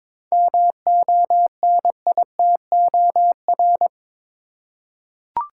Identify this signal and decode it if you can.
Morse code, heard as a keyed tone, about 22 wpm, 700 Hz. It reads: MONITOR
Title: Seven Letter Words - 22wpm